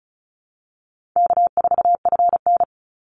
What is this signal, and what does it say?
Morse code, heard as a keyed tone, about 34 wpm, 700 Hz. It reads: K4FN
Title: Easy Call Signs Repeated 3x - 35wpm